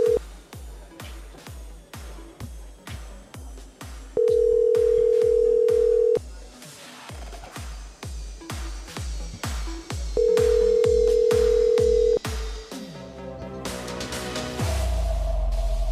Ringback tone (North America), 440 + 480 Hz.